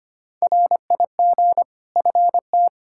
Morse code, heard as a keyed tone, about 25 wpm, 700 Hz. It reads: RIG FT
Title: Practice QSO - Exchange 2 - Info - 25wpm